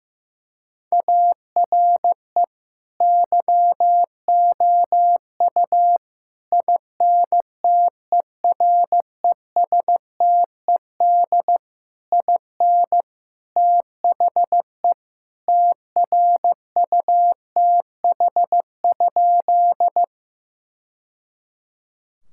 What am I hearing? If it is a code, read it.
Morse code, heard as a keyed tone, about 15 wpm, 700 Hz. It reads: ARE YOU INTERESTED IN THE TRUTH?
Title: Sentences from Top 1000 Words - Review Part 6 - 15wpm